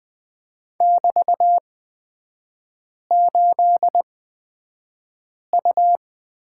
Morse code, heard as a keyed tone, about 20 wpm, spaced as for 5 wpm, 700 Hz. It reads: <BT>8U
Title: 224 Lesson 36e - BT - Three Characters 20wpm